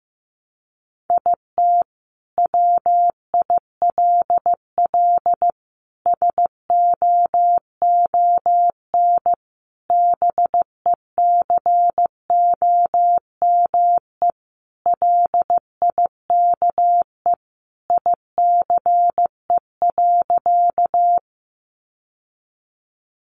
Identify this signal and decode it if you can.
Morse code, heard as a keyed tone, about 15 wpm, 700 Hz. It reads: IT WILL SOON BECOME LIKE ICE.
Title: Sentences from Top 1000 Words 15wpm